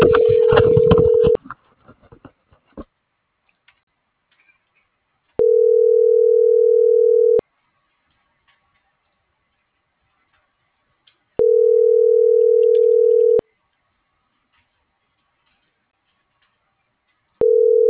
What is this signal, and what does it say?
Ringback tone (North America), 440 + 480 Hz.